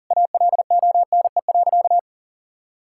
Morse code, heard as a keyed tone, about 40 wpm, 700 Hz. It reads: ALONE.
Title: Sentences from Top 1000 Words - Review Part 5 - 40wpm